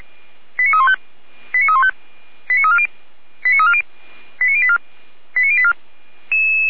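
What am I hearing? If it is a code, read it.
Five-tone selective call (ZVEI), heard as quick runs of five tones, tones of about 70 ms. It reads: 89315 89315 89340 89340 89084 89084
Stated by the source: ZVEI five-tone selective call received by SDR.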